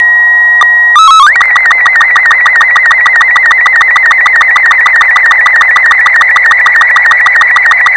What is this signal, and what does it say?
SSTV (slow-scan television), Robot 36.